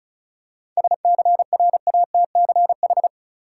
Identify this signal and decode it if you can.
Morse code, heard as a keyed tone, about 34 wpm, 700 Hz. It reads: SCRATCH
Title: Seven Letter Words - 35wpm